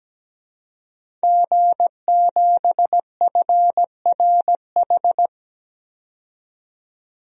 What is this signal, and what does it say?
Morse code, heard as a keyed tone, about 17 wpm, 700 Hz. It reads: G7FRH
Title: DXCC-100 Call Signs - 5 Characters - 17wpm